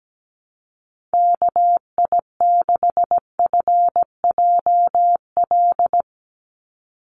Morse code, heard as a keyed tone, about 17 wpm, 700 Hz. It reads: KI6FJL
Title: USA Call Signs 17wpm